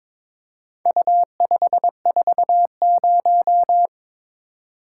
Morse code, heard as a keyed tone, about 22 wpm, 700 Hz. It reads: U540
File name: practice / 22wpm / Four Letter-Number Combinations - 22wpm